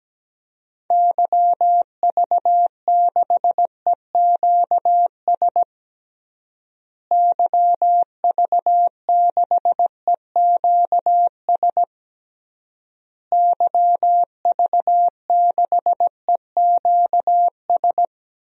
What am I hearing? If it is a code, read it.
Morse code, heard as a keyed tone, about 17 wpm, 700 Hz. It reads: YV6EQS YV6EQS YV6EQS
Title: DXCC-100 Call Signs - 6 Characters - Repeated 3x - 17wpm